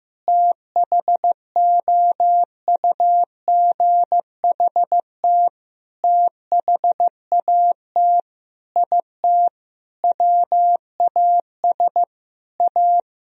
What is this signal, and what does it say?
Morse code, heard as a keyed tone, about 16 wpm, 700 Hz. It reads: THOUGHT THAT IT WAS A
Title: Sentences from Top 1000 Words - Review Part 4 - 15wpm